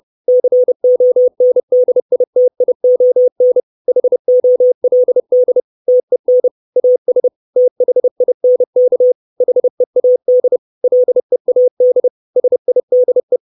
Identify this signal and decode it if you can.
Morse code, heard as a keyed tone, about 30 wpm, 500 Hz. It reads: CONDITION HOLD TEN AS THINK HEAD LEAD SIDE
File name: 632 Words 30WPM